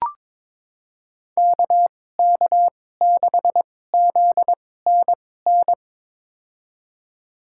Morse code, heard as a keyed tone, about 22 wpm, spaced as for 16 wpm, 700 Hz. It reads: KK6ZNN
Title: USA Call Signs - 6 Characters - 2x Character Spacing - 22wpm